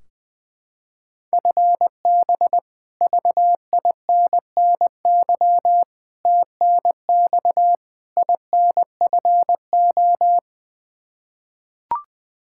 Morse code, heard as a keyed tone, about 21 wpm, 700 Hz. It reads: FB VINNY TNX INFO
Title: Practice QSO - Exchange 3 - More Info - 20wpm